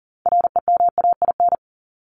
Morse code, heard as a keyed tone, about 40 wpm, 700 Hz. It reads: REMAIN
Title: ED Suffix - Top 100 Words with Base Words 40wpm